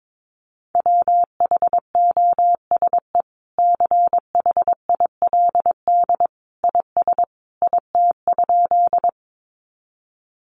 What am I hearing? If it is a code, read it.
Morse code, heard as a keyed tone, about 22 wpm, 700 Hz. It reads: WHOSE CHILD IS IT?
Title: Sentences from Top 1200 Words 22wpm